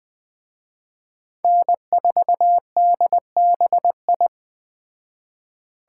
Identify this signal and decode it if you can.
Morse code, heard as a keyed tone, about 20 wpm, 700 Hz. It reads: N4DBI
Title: Speed Racing - USA 1x3 Call Signs 17wpm